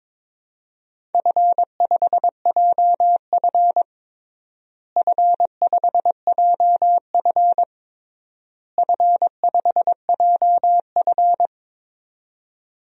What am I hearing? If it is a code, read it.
Morse code, heard as a keyed tone, about 22 wpm, 700 Hz. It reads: F5JF F5JF F5JF
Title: DXCC-100 Call Signs - 4 Characters - Repeated 3x - 22wpm